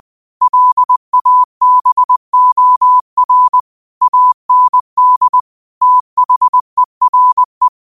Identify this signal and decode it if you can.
Morse code, heard as a keyed tone, about 20 wpm, 1000 Hz. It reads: LABOR AND THERE
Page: New quotes every day in morse code at 20 Words per minute.